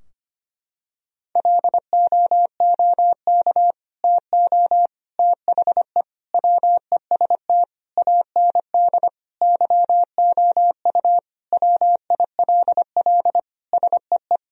Morse code, heard as a keyed tone, about 25 wpm, 700 Hz. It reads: LOOK TO THE WEST AND YOU WILL SEE
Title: Sentences from Top 1000 Words - Review Part 2 - 25wpm